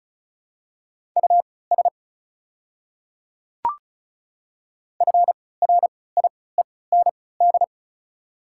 Morse code, heard as a keyed tone, about 36 wpm, spaced as for 20 wpm, 700 Hz. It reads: US FRIEND
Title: 3x Character Spacing - Top 300 Words - 35wpm